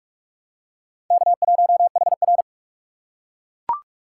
Morse code, heard as a keyed tone, about 38 wpm, 700 Hz. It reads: K1HR